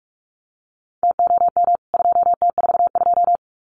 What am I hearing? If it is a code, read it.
Morse code, heard as a keyed tone, about 44 wpm, 700 Hz. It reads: TOM 2T42
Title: CWT Exchange 45wpm